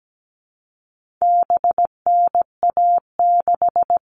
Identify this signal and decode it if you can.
Morse code, heard as a keyed tone, about 17 wpm, 700 Hz. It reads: BNA6
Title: All Cut Numbers - 4-Digit 17wpm